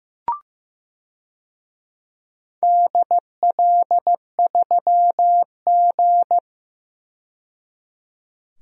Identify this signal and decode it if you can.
Morse code, heard as a keyed tone, about 15 wpm, 700 Hz. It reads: DL3G
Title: DXCC-100 Call Signs - 4 Characters - 15wpm